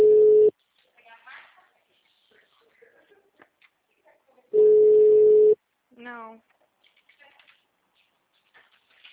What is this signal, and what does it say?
Ringback tone (Europe, 425 Hz), with speech.
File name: out- -2032- - - .243